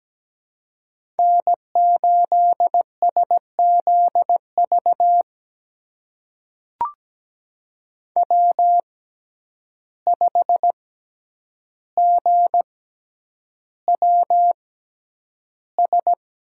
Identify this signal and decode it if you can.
Morse code, heard as a keyed tone, about 17 wpm, spaced as for 9 wpm, 700 Hz. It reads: N8SZV W5GWS
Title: USA Call Signs - 5 Characters - 6x Character Spacing - 17wpm